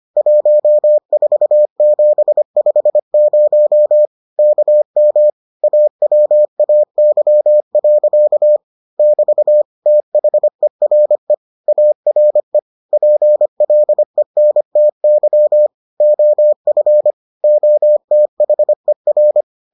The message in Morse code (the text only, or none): 14750 KM AWAY. <BT> THERE ARE PLENTY OF OTHER